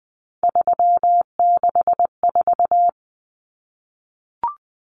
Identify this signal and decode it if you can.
Morse code, heard as a keyed tone, about 20 wpm, 700 Hz. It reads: 364
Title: Numbers - 3-Digit 20wpm